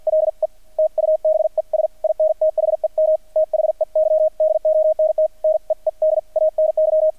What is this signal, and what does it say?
Morse code, heard as a keyed tone, about 50 wpm, spaced as for 44 wpm, 630 Hz, with radio noise. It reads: LE TUBES IN THEM THEY DONT NEED ANY
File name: archived / CRSnet